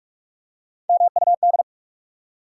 Morse code, heard as a keyed tone, about 44 wpm, 700 Hz. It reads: MUD
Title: Three Letter Words - 45wpm